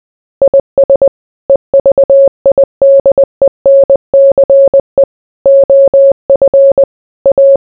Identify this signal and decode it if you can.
Morse code, heard as a keyed tone, about 20 wpm, 550 Hz. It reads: IS EVIDENCE OF A